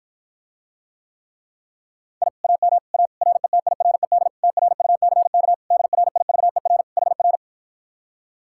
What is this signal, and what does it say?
Morse code, heard as a keyed tone, about 53 wpm, 700 Hz. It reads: I AM A RETIRED TRUCK DRIVER HR